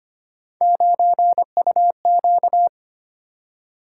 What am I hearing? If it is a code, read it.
Morse code, heard as a keyed tone, about 25 wpm, 700 Hz. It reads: 9UQ
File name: Three Letter-Number Combinations - 25wpm